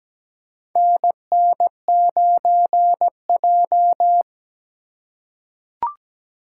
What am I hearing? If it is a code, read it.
Morse code, heard as a keyed tone, about 17 wpm, 700 Hz. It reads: NN9J
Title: USA Call Signs - 4 Characters - 4x Character Spacing - 17wpm